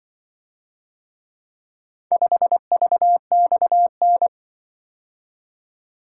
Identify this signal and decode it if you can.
Morse code, heard as a keyed tone, about 24 wpm, 700 Hz. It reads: 5VXN
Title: Speed-Racing - Four Letter-Number Combinations - 20wpm